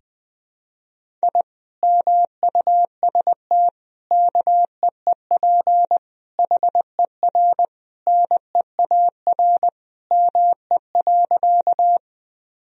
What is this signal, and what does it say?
Morse code, heard as a keyed tone, about 20 wpm, 700 Hz. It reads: I MUST KEEP HER NEAR ME.
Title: Sentences from Top 1000 Words - Review Part 1 - 20wpm